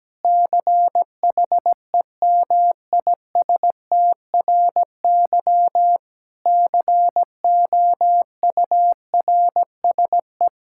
Morse code, heard as a keyed tone, about 17 wpm, 700 Hz. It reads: CHEMISTRY COURSE
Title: Sentences from Top 1000 Words - Review Part 6 - 17wpm